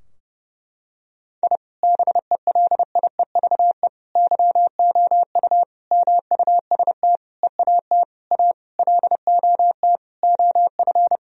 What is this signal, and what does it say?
Morse code, heard as a keyed tone, about 32 wpm, 700 Hz. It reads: I BELIEVE YOU MUST EAT A LOT OF
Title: Sentences from Top 1000 Words - Review Part 3 - 30wpm